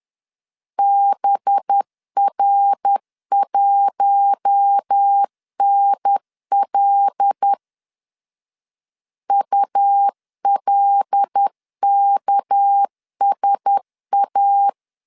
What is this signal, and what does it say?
Morse code, heard as a keyed tone, about 10 wpm, 800 Hz. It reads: BR1NL ULKSA
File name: CW / DET / 1 9 1500C S A 10wpm 4511